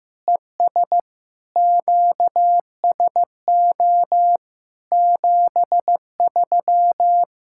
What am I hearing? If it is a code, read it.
Morse code, heard as a keyed tone, about 15 wpm, 700 Hz. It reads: ES QSO 73
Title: Practice QSO - Exchange 4 - Ending - 15wpm